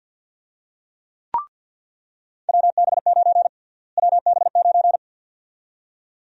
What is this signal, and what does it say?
Morse code, heard as a keyed tone, about 46 wpm, 700 Hz. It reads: WB9 WB9
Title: USA Call Sign Prefixes - Repeated 2x - 50wpm